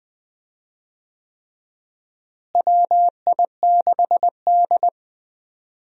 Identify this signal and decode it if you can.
Morse code, heard as a keyed tone, about 20 wpm, 700 Hz. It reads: WI6D
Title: USA Call Signs - 4 Characters - Repeated 3x - 20wpm